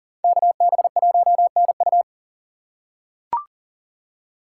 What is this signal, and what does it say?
Morse code, heard as a keyed tone, about 40 wpm, 700 Hz. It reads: KB1NU
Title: USA Call Signs - 2x Character Spacing - 40wpm